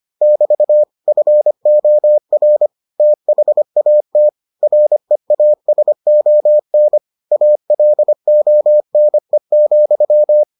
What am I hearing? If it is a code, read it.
Morse code, heard as a keyed tone, about 25 wpm, 600 Hz. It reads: <BT> FOR THAT REASON ALONE,